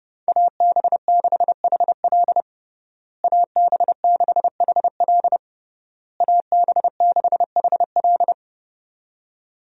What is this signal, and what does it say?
Morse code, heard as a keyed tone, about 30 wpm, 700 Hz. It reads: AB6HL AB6HL AB6HL
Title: USA Call Signs - 5 Characters - Repeated 3x - 30wpm